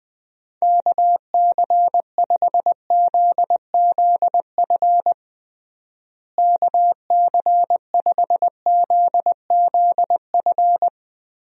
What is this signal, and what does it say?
Morse code, heard as a keyed tone, about 20 wpm, 700 Hz. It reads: KC5ZZF KC5ZZF
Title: USA Call Signs - 6 Characters - Repeated 3x - 20wpm